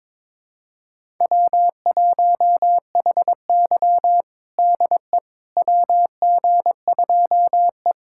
Morse code, heard as a keyed tone, about 22 wpm, 700 Hz. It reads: W1HY DE WG2E
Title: USA Callsign DE Callsign - 4 Characters - 22wpm